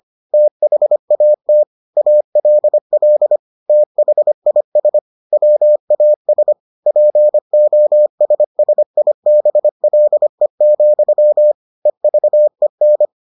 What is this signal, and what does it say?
Morse code, heard as a keyed tone, about 25 wpm, 600 Hz. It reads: THAT ALL THIS WAS POSSIBLE, EVEN